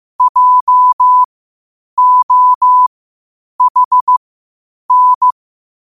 Morse code, heard as a keyed tone, about 15 wpm, spaced as for 8 wpm, 1000 Hz. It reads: JOHN
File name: qotdfiles / 8wpm